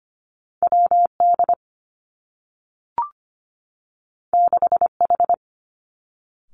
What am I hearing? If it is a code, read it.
Morse code, heard as a keyed tone, about 25 wpm, 700 Hz. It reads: W D 6 H
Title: Two Challenging Letter-Number Combinations 25wpm